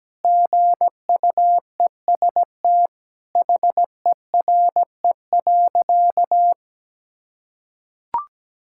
Morse code, heard as a keyed tone, about 17 wpm, 700 Hz. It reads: GUEST HERE.
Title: Sentences from Top 1600 Words 17wpm